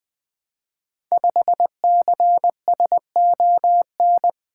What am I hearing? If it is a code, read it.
Morse code, heard as a keyed tone, about 20 wpm, 700 Hz. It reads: 5CSON
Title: Code Groups - 5 Letter:Numbers - 20wpm